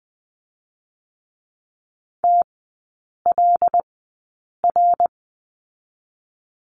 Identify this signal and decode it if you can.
Morse code, heard as a keyed tone, about 20 wpm, spaced as for 8 wpm, 700 Hz. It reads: TLR
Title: 087 Lesson 15e - Number 5 - Three Characters 20wpm